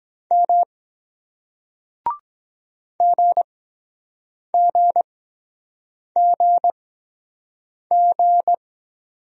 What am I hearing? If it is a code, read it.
Morse code, heard as a keyed tone, about 23 wpm, spaced as for 7 wpm, 700 Hz. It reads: M GGGG